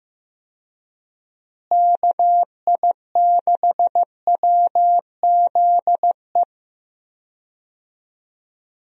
Morse code, heard as a keyed tone, about 15 wpm, 700 Hz. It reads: KI6WZE
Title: USA Call Signs - 6 Characters - Repeated 2x - 15wpm